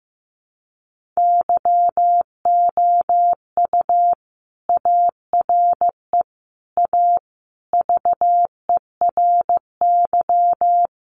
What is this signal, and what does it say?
Morse code, heard as a keyed tone, about 15 wpm, 700 Hz. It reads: YOU ARE A VERY
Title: Sentences from Top 1100 Words 15wpm